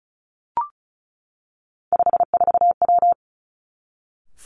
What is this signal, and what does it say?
Morse code, heard as a keyed tone, about 36 wpm, 700 Hz. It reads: 54W
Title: Three Challenging Letter-Number Combinations 35wpm